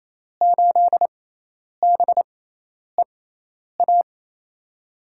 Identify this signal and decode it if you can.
Morse code, heard as a keyed tone, about 28 wpm, spaced as for 10 wpm, 700 Hz. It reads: 8BEA
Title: USA Call Signs - 6 Characters - 6x Character Spacing - 28wpm